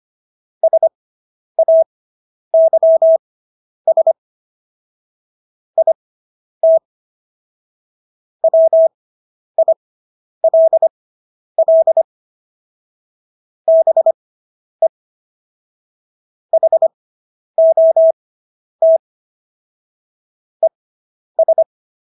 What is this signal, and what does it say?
Morse code, heard as a keyed tone, about 25 wpm, spaced as for 10 wpm, 650 Hz. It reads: SAYS IT WILL BE HOT ES